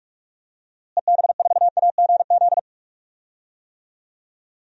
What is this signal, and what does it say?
Morse code, heard as a keyed tone, about 46 wpm, 700 Hz. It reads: EB4AGZ
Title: DXCC-100 Call Signs - 6 Characters - Repeated 3x - 45wpm